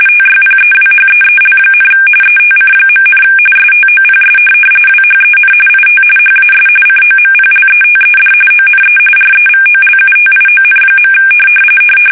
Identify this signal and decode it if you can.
RTTY, about 45.45 baud. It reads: TYPEWHEEL, ETC., STOPS WHEN IT HAS COMPLETED ONE REVOLUTION.